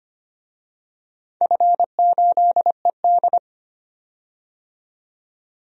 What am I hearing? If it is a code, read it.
Morse code, heard as a keyed tone, about 25 wpm, 700 Hz. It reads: F8ED
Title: DXCC-100 Call Signs - 4 Characters - 25wpm